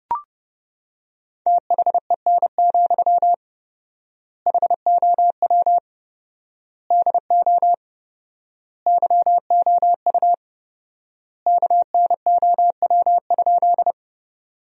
Morse code, heard as a keyed tone, about 30 wpm, 700 Hz. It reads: THEN, HOW DO YOU KNOW?
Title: 4x Word-Spacing - Sentences from Top 100 Words - 30wpm